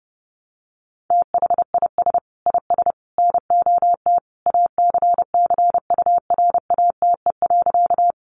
Morse code, heard as a keyed tone, about 30 wpm, 700 Hz. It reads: THIS IS NOT ACCURATE.
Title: Sentences from Top 1200 Words 30wpm